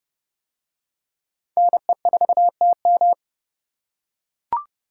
Morse code, heard as a keyed tone, about 29 wpm, 700 Hz. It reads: NE4TM